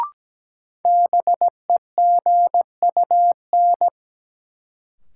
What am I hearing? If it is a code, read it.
Morse code, heard as a keyed tone, about 17 wpm, 700 Hz. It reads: BEGUN
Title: Sentences from Top 1000 Words - Review Part 6 - New Word Drill - 17wpm